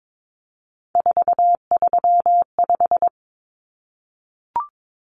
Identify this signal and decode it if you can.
Morse code, heard as a keyed tone, about 22 wpm, 700 Hz. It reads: 435
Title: Common Cut Numbers - 3-Digit 22wpm